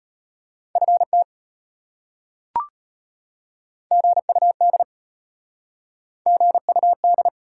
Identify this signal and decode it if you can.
Morse code, heard as a keyed tone, about 33 wpm, 700 Hz. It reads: FT GUD GUD